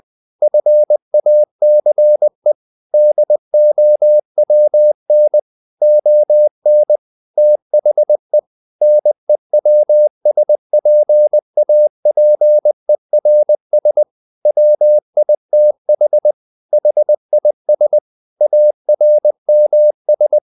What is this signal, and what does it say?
Morse code, heard as a keyed tone, about 20 wpm, 600 Hz. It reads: FACE DOWN ON THE NEWSPAPERS WITH HIS ARMS
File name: the old man and the sea part 2 20wpm